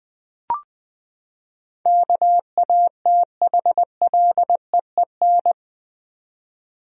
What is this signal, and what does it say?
Morse code, heard as a keyed tone, about 20 wpm, 700 Hz. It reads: KATHLEEN
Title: USA Common Names - 20wpm